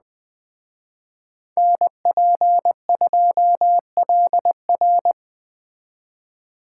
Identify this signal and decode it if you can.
Morse code, heard as a keyed tone, about 20 wpm, 700 Hz. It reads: NP2LR